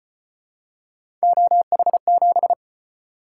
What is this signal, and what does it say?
Morse code, heard as a keyed tone, about 34 wpm, 700 Hz. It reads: OH7